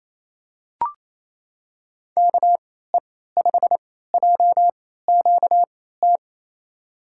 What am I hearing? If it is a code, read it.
Morse code, heard as a keyed tone, about 28 wpm, spaced as for 16 wpm, 700 Hz. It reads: KE5JQT
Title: USA Call Signs - 6 Characters - 3x Character Spacing - 28wpm